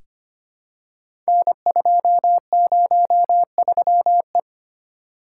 Morse code, heard as a keyed tone, about 25 wpm, 690 Hz. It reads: N203E
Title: Code Groups - 5 Letter:Numbers - 25wpm